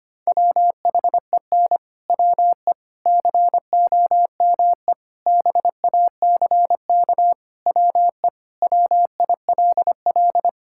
Morse code, heard as a keyed tone, about 25 wpm, 700 Hz. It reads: WHEN WE COME BACK WE WILL
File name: practice / 25wpm / Sentences with 4-Word Phrases - 25wpm